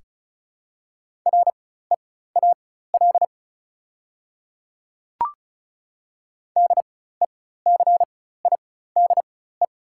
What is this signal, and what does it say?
Morse code, heard as a keyed tone, about 34 wpm, spaced as for 16 wpm, 690 Hz. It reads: REAL DECIDE
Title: 4x Character Spacing - Top 500 Words - 35wpm